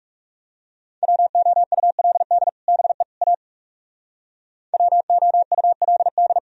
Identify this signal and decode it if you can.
Morse code, heard as a keyed tone, about 37 wpm, 700 Hz. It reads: WOULD BE A WOULD
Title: Speed-Racing - Sets of 3 Words 30wpm